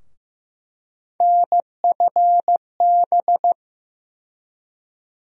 Morse code, heard as a keyed tone, about 15 wpm, 700 Hz. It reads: NFB
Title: USA Call Sign Postfixes - 3 Letters 15wpm